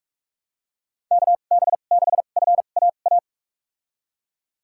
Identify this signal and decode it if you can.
Morse code, heard as a keyed tone, about 42 wpm, spaced as for 31 wpm, 700 Hz. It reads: KB6FAA
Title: USA Call Signs - 2x Character Spacing - 45wpm